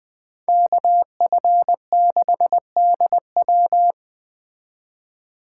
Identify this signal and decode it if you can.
Morse code, heard as a keyed tone, about 20 wpm, 700 Hz. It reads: KF6DW